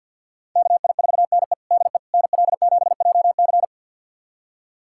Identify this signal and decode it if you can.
Morse code, heard as a keyed tone, about 52 wpm, 700 Hz. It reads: KI4NE DE NL7JC
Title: USA Callsign DE Callsign 50wpm